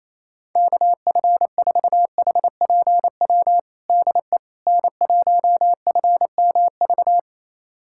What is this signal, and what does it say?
Morse code, heard as a keyed tone, about 28 wpm, 700 Hz. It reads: KF4HPW DE N1FMV
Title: USA Callsign DE Callsign 28wpm